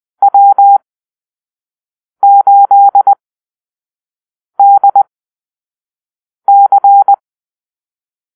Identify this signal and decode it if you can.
Morse code, heard as a keyed tone, about 20 wpm, spaced as for 5 wpm, 800 Hz. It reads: W8DC
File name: arrl / Call-Sign Drill-1